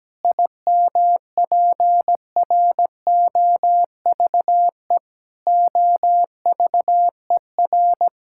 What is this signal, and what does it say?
Morse code, heard as a keyed tone, about 17 wpm, 700 Hz. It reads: IMPROVE OVER
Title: Sentences from Top 1000 Words - Review Part 5 - 17wpm